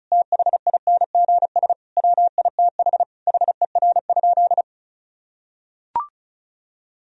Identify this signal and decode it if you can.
Morse code, heard as a keyed tone, about 36 wpm, 700 Hz. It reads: THINGS WITH HER?
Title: Sentences from Top 1000 Words - Review Part 4 - 35wpm